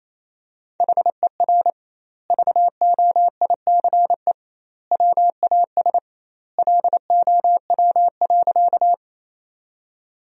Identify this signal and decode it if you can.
Morse code, heard as a keyed tone, about 28 wpm, 700 Hz. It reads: HER VOICE WAS LOW.
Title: 2x Word-Spacing - Sentences from Top 500 Words - 28wpm